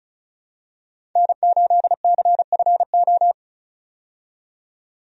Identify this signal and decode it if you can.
Morse code, heard as a keyed tone, about 34 wpm, 700 Hz. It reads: N8CFO
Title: USA Call Signs - 5 Characters - 35wpm